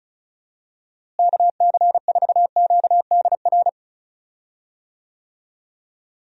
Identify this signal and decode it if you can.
Morse code, heard as a keyed tone, about 36 wpm, 700 Hz. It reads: KC4QDR
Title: USA Call Signs - 6 Characters - Repeated 2x - 35wpm